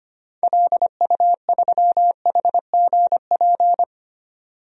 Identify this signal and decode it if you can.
Morse code, heard as a keyed tone, about 25 wpm, 700 Hz. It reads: LU3HGP
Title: DXCC-100 Call Signs - 6 Characters - 25wpm